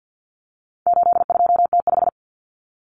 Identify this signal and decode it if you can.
Morse code, heard as a keyed tone, about 50 wpm, 700 Hz. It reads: 82T5
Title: Common Cut Numbers - 4-Digit 50wpm